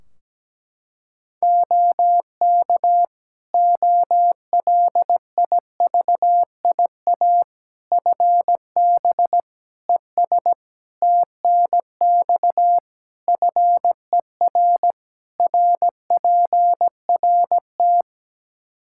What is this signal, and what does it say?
Morse code, heard as a keyed tone, about 17 wpm, 700 Hz. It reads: OK OLIVIA FB ES TNX FER RPRT
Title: Practice QSO - Exchange 2 - Info - 17wpm